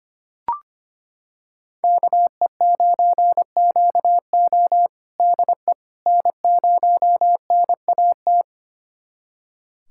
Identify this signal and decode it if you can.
Morse code, heard as a keyed tone, about 25 wpm, 700 Hz. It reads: KE9QO DE N0NAT
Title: USA Callsign DE Callsign - 5 Characters - 25wpm